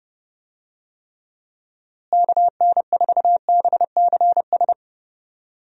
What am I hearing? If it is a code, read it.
Morse code, heard as a keyed tone, about 32 wpm, 700 Hz. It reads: KN4BCS
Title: USA Call Signs - 6 Characters - Repeated 2x - 30wpm